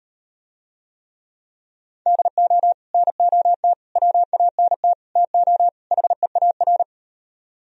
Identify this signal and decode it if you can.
Morse code, heard as a keyed tone, about 38 wpm, 700 Hz. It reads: DO NOT WANT TO HEAR